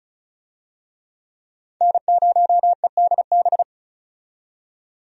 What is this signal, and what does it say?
Morse code, heard as a keyed tone, about 36 wpm, 700 Hz. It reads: N0EDB
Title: USA Call Signs - 5 Characters - 35wpm